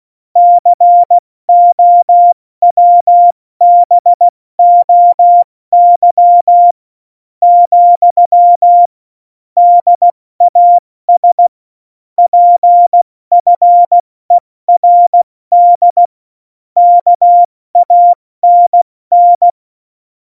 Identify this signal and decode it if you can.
Morse code, heard as a keyed tone, about 16 wpm, 700 Hz. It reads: COWBOY , DAS PFERD KANN
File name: Morsekurs / Witze / Witz 06. Oktober 2020, 70 F80